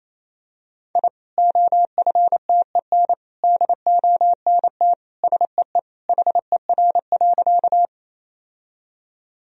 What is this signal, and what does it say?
Morse code, heard as a keyed tone, about 28 wpm, 700 Hz. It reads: I OFTEN DONT SEE HER.
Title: Sentences from Top 1000 Words - Review Part 1 - 28wpm